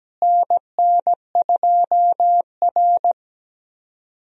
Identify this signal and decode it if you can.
Morse code, heard as a keyed tone, about 17 wpm, 700 Hz. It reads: NN2R
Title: USA Call Signs - 4 Characters - Repeated 2x - 17wpm